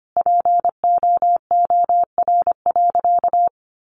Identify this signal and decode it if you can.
Morse code, heard as a keyed tone, about 25 wpm, 700 Hz. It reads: POOR.
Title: Sentences from Top 1000 Words 25wpm